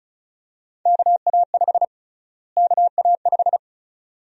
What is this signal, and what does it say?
Morse code, heard as a keyed tone, about 34 wpm, 700 Hz. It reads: KA5 KA5
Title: USA Call Sign Prefixes - Repeated 2x - 35wpm